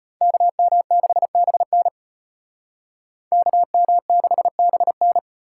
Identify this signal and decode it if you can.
Morse code, heard as a keyed tone, about 30 wpm, 700 Hz. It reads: KM6BN KM6BN
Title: Speed Racing - USA 2x2 Call Signs 25wpm